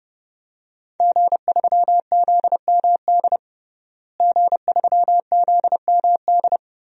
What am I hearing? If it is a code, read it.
Morse code, heard as a keyed tone, about 30 wpm, 700 Hz. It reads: G3ZMD G3ZMD
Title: DXCC-100 Call Signs - 5 Characters - Repeated 2x - 30wpm